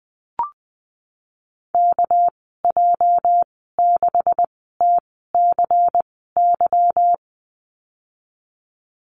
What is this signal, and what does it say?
Morse code, heard as a keyed tone, about 19 wpm, spaced as for 14 wpm, 700 Hz. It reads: KJ6TCY
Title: 213 Lesson 34g - Number 8 - Call Signs 20wpm